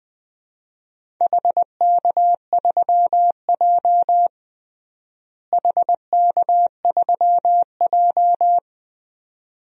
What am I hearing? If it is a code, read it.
Morse code, heard as a keyed tone, about 21 wpm, 700 Hz. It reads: HK3J HK3J
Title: DXCC-100 Call Signs - 4 Characters - Repeated 2x - 20wpm